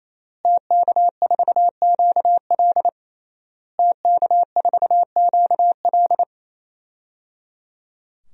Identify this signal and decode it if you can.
Morse code, heard as a keyed tone, about 27 wpm, 700 Hz. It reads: TK4QL TK4QL
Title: DXCC-100 Call Signs - 5 Characters - Repeated 3x - 28wpm